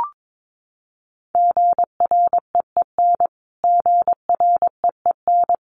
Morse code, heard as a keyed tone, about 22 wpm, 700 Hz. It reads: GREEN GREEN
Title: Top 500 Words Repeated 22wpm